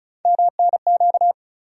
Morse code, heard as a keyed tone, about 36 wpm, 700 Hz. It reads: MNQ
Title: USA Call Sign Postfixes - 3 Letters 35wpm